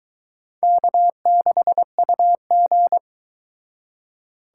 Morse code, heard as a keyed tone, about 23 wpm, 700 Hz. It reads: K6UG